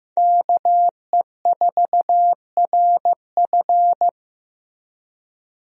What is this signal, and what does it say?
Morse code, heard as a keyed tone, about 15 wpm, 700 Hz. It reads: KE4RF
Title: USA Call Signs - 5 Characters - Repeated 3x - 15wpm